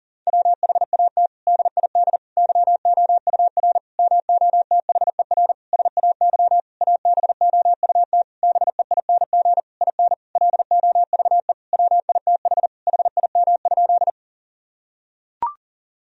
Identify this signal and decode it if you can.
Morse code, heard as a keyed tone, about 40 wpm, 700 Hz. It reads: WHAT DID YOUR MOTHER SAY ABOUT BEING IN LOVE WITH HIM?
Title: Sentences from Top 1000 Words - Review Part 3 - 40wpm